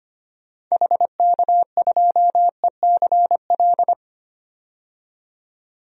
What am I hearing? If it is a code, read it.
Morse code, heard as a keyed tone, about 25 wpm, 700 Hz. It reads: HK2ECL
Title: DXCC-100 Call Signs - 6 Characters - 25wpm